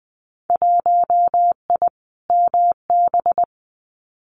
Morse code, heard as a keyed tone, about 20 wpm, 700 Hz. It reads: 1I MB
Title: Winter Field Day Exchanges Encore 20wpm